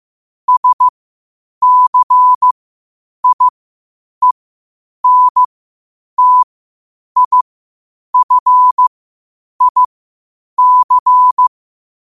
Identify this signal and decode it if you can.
Morse code, heard as a keyed tone, about 15 wpm, spaced as for 8 wpm, 1000 Hz. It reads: SCIENTIFIC
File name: qotdfiles / 8wpm